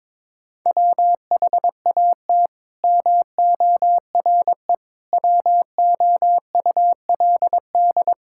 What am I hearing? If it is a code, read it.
Morse code, heard as a keyed tone, about 22 wpm, 700 Hz. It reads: WHAT MORE WOULD
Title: Sentences from Top 1000 Words - Review Part 6 - 22wpm